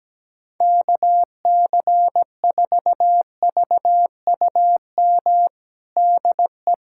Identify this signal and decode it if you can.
Morse code, heard as a keyed tone, about 17 wpm, 700 Hz. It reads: KC4VUM DE
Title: USA Callsign DE Callsign - 6 Characters - 17wpm